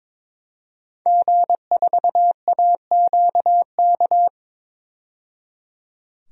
Morse code, heard as a keyed tone, about 22 wpm, 700 Hz. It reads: G4AQK
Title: DXCC-100 Call Signs - 5 Characters - 22wpm